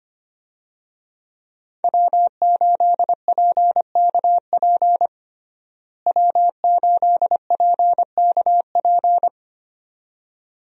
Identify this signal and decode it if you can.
Morse code, heard as a keyed tone, about 25 wpm, 700 Hz. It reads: W8PKP W8PKP
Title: USA Call Signs - 5 Characters - Repeated 2x - 25wpm